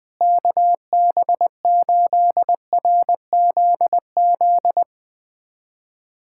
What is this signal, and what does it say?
Morse code, heard as a keyed tone, about 20 wpm, 700 Hz. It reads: KB8RZZ
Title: USA Callsign DE Callsign 20wpm